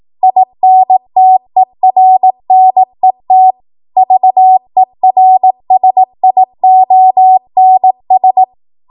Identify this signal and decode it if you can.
Morse code, heard as a keyed tone, about 18 wpm, 750 Hz. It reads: INTERNET VERSIONS
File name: Morse / 18WPM